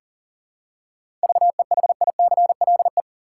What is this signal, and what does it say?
Morse code, heard as a keyed tone, about 40 wpm, 700 Hz. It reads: VEHICLE
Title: Seven Letter Words - 40wpm